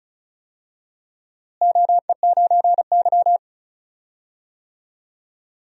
Morse code, heard as a keyed tone, about 36 wpm, 700 Hz. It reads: OE9Y
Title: DXCC-100 Call Signs - 4 Characters - Repeated 2x - 35wpm